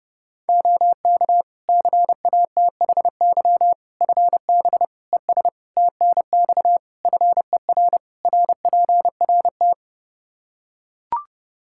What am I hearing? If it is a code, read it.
Morse code, heard as a keyed tone, about 30 wpm, 700 Hz. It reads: OK CATHY FB ES TNX FER RPRT
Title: K1USN SST QSO Practice 30wpm